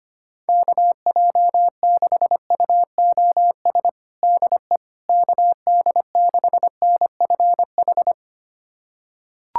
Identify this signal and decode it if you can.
Morse code, heard as a keyed tone, about 25 wpm, 700 Hz. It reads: KJ6UOS DE KD6NFH
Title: USA Callsign DE Callsign - 6 Characters - 25wpm